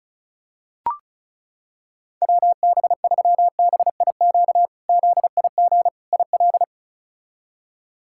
Morse code, heard as a keyed tone, about 36 wpm, 700 Hz. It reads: WB3BIQ ZIG IL
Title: K1USN SST QSO Practice - 35wpm